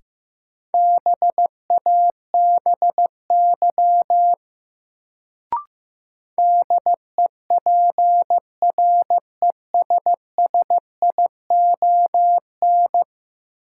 Morse code, heard as a keyed tone, about 15 wpm, 700 Hz. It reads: BABY DEPRESSION
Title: Sentences from Top 1300 Words - New Word Drill - 15wpm